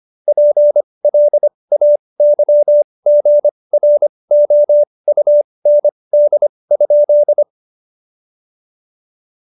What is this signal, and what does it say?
Morse code, heard as a keyed tone, about 25 wpm, spaced as for 20 wpm, 590 Hz. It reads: PLAYGROUND?
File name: Jokes-25-char-20-eff